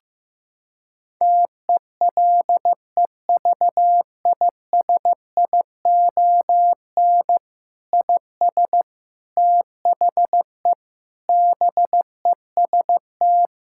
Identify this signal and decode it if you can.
Morse code, heard as a keyed tone, about 16 wpm, 700 Hz. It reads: TELEVISION IS THE BEST
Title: Sentences from Top 1000 Words - Review Part 6 - 15wpm